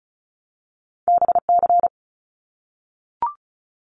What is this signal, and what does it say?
Morse code, heard as a keyed tone, about 36 wpm, 700 Hz. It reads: BC
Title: Canadian Provinces and Territories 35wpm